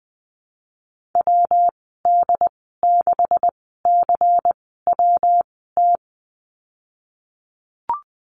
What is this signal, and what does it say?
Morse code, heard as a keyed tone, about 20 wpm, 700 Hz. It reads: W D 6 C W T
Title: 154 Lesson 25g - Letter p - Call Signs 20wpm